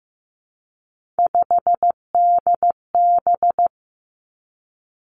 Morse code, heard as a keyed tone, about 15 wpm, 700 Hz. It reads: 5DB
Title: Three Challenging Letter-Number Combinations 15wpm